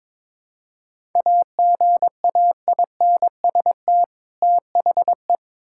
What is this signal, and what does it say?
Morse code, heard as a keyed tone, about 22 wpm, 700 Hz. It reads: AGAINST THE
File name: Build A Phrase - 2-Word Phrases - 22wpm